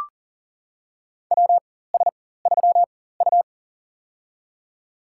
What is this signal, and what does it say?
Morse code, heard as a keyed tone, about 40 wpm, spaced as for 19 wpm, 700 Hz. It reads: WS3U
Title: USA Call Signs - 4 Characters - 4x Character Spacing - 40wpm